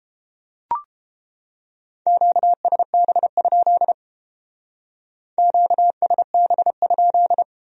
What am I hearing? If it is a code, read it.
Morse code, heard as a keyed tone, about 30 wpm, 700 Hz. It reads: QSB? QSB?